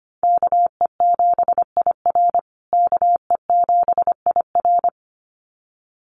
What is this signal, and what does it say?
Morse code, heard as a keyed tone, about 25 wpm, 700 Hz. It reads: KE7IR KE7IR
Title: USA Call Signs Repeated 2x 25wpm